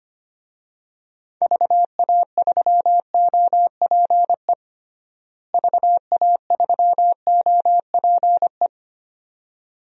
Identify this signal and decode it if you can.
Morse code, heard as a keyed tone, about 25 wpm, 700 Hz. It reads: VA3OPE VA3OPE
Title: DXCC-100 Call Signs - 6 Characters - Repeated 2x - 25wpm